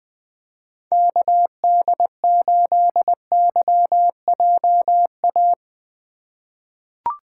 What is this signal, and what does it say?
Morse code, heard as a keyed tone, about 20 wpm, 700 Hz. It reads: KD8YJA
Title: USA Call Signs - 2x Character Spacing - 20wpm